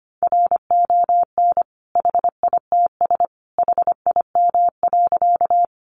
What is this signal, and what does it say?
Morse code, heard as a keyed tone, about 25 wpm, 700 Hz. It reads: RON HITS HIM.
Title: Additional Easy Phrases 25wpm